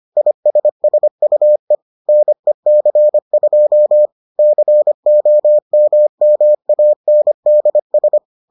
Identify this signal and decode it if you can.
Morse code, heard as a keyed tone, about 25 wpm, 600 Hz. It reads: ISSUE NEC2 COMMANDS